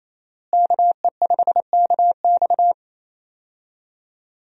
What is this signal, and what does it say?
Morse code, heard as a keyed tone, about 28 wpm, 700 Hz. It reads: KE5KX
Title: USA Call Signs - 5 Characters - Repeated 3x - 28wpm